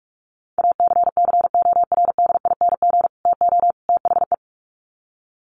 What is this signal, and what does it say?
Morse code, heard as a keyed tone, about 44 wpm, 700 Hz. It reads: ACCORDING TO THE
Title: Sets of 3 Words 45wpm